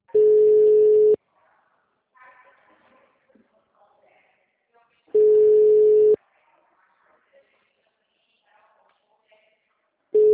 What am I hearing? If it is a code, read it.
Ringback tone (Europe, 425 Hz), with speech.